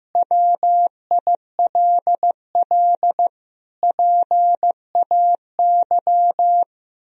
Morse code, heard as a keyed tone, about 15 wpm, 700 Hz. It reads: WILL PAY
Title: Sentences from Top 1000 Words - Review Part 4 - 15wpm